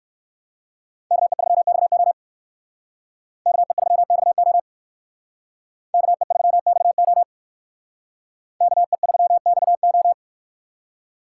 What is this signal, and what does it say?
Morse code, heard as a keyed tone, about 60 wpm, 700 Hz. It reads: KE3XO KE3XO KE3XO KE3XO